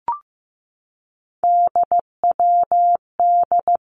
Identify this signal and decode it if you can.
Morse code, heard as a keyed tone, about 15 wpm, 700 Hz. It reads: DWD
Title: Three Challenging Letter-Number Combinations 15wpm